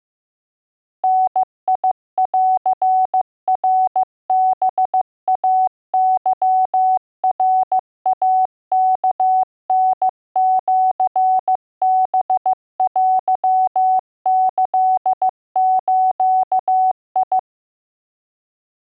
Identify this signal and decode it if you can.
Morse code (Japanese Wabun code), heard as a keyed tone, about 15 wpm, 750 Hz. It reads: ダンナハイケナイワタシハテキズ